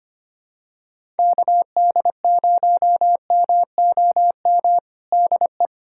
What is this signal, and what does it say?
Morse code, heard as a keyed tone, about 25 wpm, 700 Hz. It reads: KD0MOM DE
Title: USA Callsign DE Callsign - 6 Characters - 25wpm